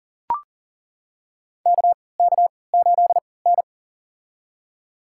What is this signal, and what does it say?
Morse code, heard as a keyed tone, about 40 wpm, spaced as for 23 wpm, 700 Hz. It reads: KK8N
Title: USA Call Signs - 3x Character Spacing - 40wpm